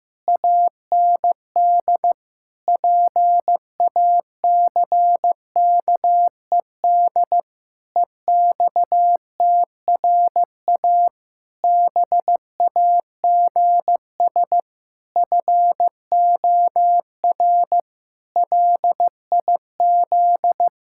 Morse code, heard as a keyed tone, about 16 wpm, 700 Hz. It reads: AND PACKED EXTRA BAGS FOR LIZ
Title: Pangram Challenge v2 - 15wpm